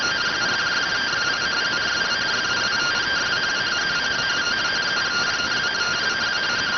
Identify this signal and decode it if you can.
Two-tone teleprinter signal, about 170 Hz shift.